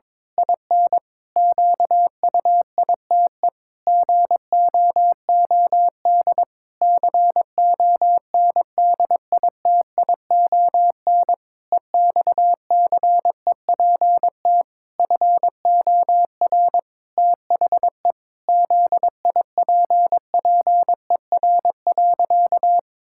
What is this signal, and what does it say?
Morse code, heard as a keyed tone, about 22 wpm, 700 Hz. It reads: IN QUITE GOOD CONDITION EXCEPT FOR THE ZIPPER.
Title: Pangram Challenge v2 - 22wpm